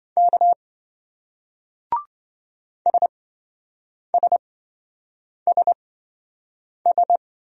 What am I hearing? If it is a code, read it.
Morse code, heard as a keyed tone, about 27 wpm, spaced as for 7 wpm, 700 Hz. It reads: K SSSS